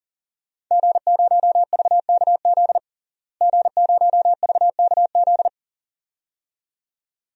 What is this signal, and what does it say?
Morse code, heard as a keyed tone, about 40 wpm, 700 Hz. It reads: G0VKZ G0VKZ
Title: DXCC-100 Call Signs - 5 Characters - Repeated 2x - 40wpm